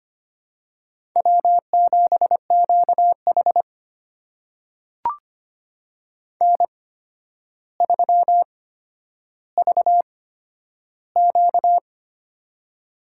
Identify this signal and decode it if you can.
Morse code, heard as a keyed tone, about 25 wpm, spaced as for 11 wpm, 700 Hz. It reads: W7QH N3VQ
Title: USA Call Signs - 4 Characters - 8x Character Spacing - 25wpm